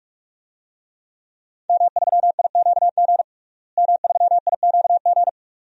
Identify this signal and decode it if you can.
Morse code, heard as a keyed tone, about 46 wpm, 700 Hz. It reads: M3IQG M3IQG
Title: DXCC-100 Call Signs - 5 Characters - Repeated 2x - 45wpm